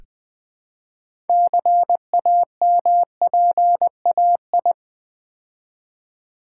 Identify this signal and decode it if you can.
Morse code, heard as a keyed tone, about 20 wpm, 700 Hz. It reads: CAMPAI
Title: Build A Word - 3 Letters At A Time - 8-Letter Words - 20wpm